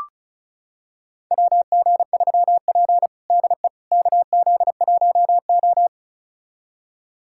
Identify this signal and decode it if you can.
Morse code, heard as a keyed tone, about 34 wpm, 700 Hz. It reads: WG3P DE KZ1O
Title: USA Callsign DE Callsign - 4 Characters - 35wpm